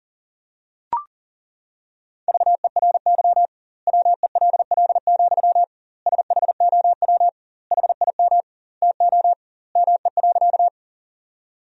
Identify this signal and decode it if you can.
Morse code, heard as a keyed tone, about 40 wpm, 700 Hz. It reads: VERY WELL, SHOW HIM TO ME.
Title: 2x Word-Spacing - Sentences from Top 200 Words - 40wpm